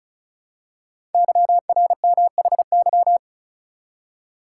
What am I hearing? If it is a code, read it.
Morse code, heard as a keyed tone, about 36 wpm, 690 Hz. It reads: YRMHY
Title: Code Groups - 5 Letters - 35wpm